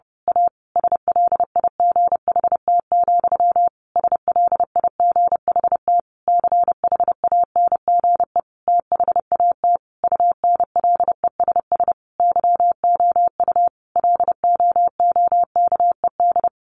Morse code, heard as a keyed tone, about 30 wpm, 700 Hz. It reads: A SLIGHT, SLIGHT CHANGE THAT UNLESS YOU LOOKED
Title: A Sound Of Thunder 30wpm